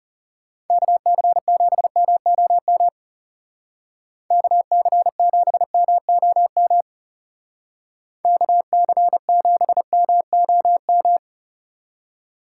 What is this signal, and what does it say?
Morse code, heard as a keyed tone, about 37 wpm, 700 Hz. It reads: KC7MOM KC7MOM KC7MOM
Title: Speed Racing - USA 2x3 Call Signs 30wpm